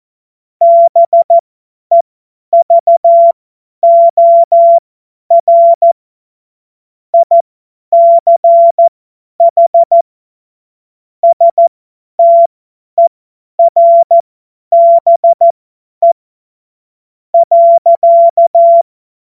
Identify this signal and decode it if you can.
Morse code, heard as a keyed tone, about 14 wpm, spaced as for 9 wpm, 680 Hz. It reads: BEVOR ICH STERBE.
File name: Morsekurs / Witze / Witz 16. Februar 2021, 50 F70